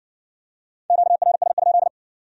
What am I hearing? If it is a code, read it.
Morse code, heard as a keyed tone, about 54 wpm, 700 Hz. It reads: QRS?